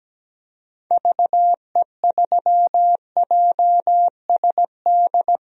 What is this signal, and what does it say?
Morse code, heard as a keyed tone, about 17 wpm, 700 Hz. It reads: VE3JSD
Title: DXCC-100 Call Signs - 6 Characters - Repeated 3x - 17wpm